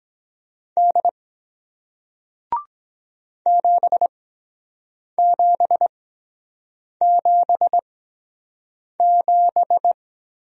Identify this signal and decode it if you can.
Morse code, heard as a keyed tone, about 23 wpm, spaced as for 7 wpm, 700 Hz. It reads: D 7777